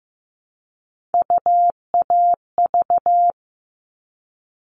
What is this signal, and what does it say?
Morse code, heard as a keyed tone, about 15 wpm, 700 Hz. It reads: UAV
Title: All Cut Numbers - 3-Digit 15wpm